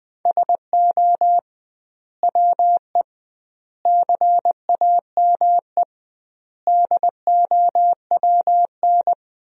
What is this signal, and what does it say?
Morse code, heard as a keyed tone, about 20 wpm, 700 Hz. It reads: SO WE CAME DOWN
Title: 2x Word-Spacing - Sentences from Top 400 Words - 20wpm